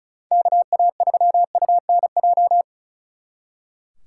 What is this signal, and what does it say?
Morse code, heard as a keyed tone, about 43 wpm, 700 Hz. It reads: KA3UNJ
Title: Speed Racing - USA Call Signs 35wpm